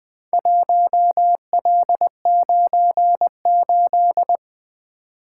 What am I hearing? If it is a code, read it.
Morse code, heard as a keyed tone, about 20 wpm, 700 Hz. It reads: 1L98
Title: Four Letter-Number Combinations - 20wpm